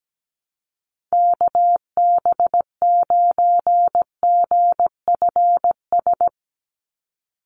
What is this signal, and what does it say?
Morse code, heard as a keyed tone, about 17 wpm, 700 Hz. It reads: KB9GFS
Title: USA Call Signs 17wpm